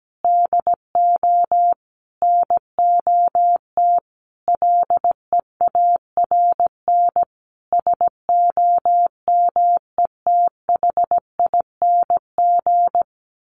Morse code, heard as a keyed tone, about 17 wpm, 700 Hz. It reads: DO NOT LEARN SOMETHING
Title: Sentences from Top 1400 Words 17wpm